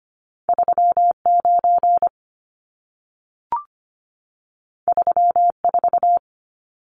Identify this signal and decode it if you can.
Morse code, heard as a keyed tone, about 25 wpm, 700 Hz. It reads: 39 34
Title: Numbers - 2-Digit 25wpm